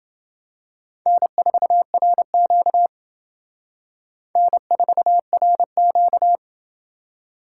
Morse code, heard as a keyed tone, about 27 wpm, 700 Hz. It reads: N4RQ N4RQ